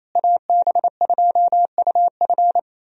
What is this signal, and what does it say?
Morse code, heard as a keyed tone, about 28 wpm, 700 Hz. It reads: AB2UF
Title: USA Call Signs - 5 Characters - Repeated 2x - 28wpm